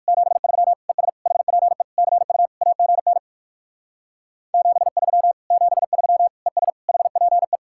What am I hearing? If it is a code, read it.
Morse code, heard as a keyed tone, about 59 wpm, 700 Hz. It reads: 73 ES HPE CU AGN 73 73 ES HPE